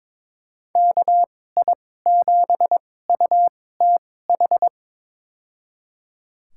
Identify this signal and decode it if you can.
Morse code, heard as a keyed tone, about 22 wpm, spaced as for 16 wpm, 700 Hz. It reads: KI7UTH
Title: USA Call Signs - 2x Character Spacing - 22wpm